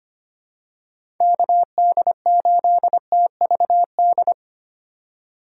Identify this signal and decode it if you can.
Morse code, heard as a keyed tone, about 25 wpm, 700 Hz. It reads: KD8TVD
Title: USA Call Signs - 6 Characters - 4x Character Spacing - 25wpm